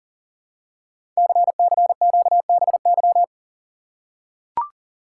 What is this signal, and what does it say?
Morse code, heard as a keyed tone, about 40 wpm, 700 Hz. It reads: CCQBY
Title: Code Groups - 5 Letters - 40wpm